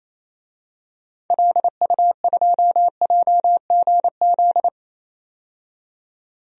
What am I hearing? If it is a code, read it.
Morse code, heard as a keyed tone, about 28 wpm, 700 Hz. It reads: LU2JGZ
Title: DXCC-100 Call Signs - 6 Characters - 28wpm